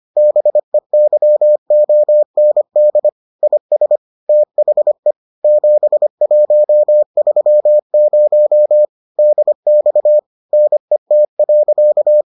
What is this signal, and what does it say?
Morse code, heard as a keyed tone, about 25 wpm, 600 Hz. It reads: BEYOND IS THE 7130 DX NET.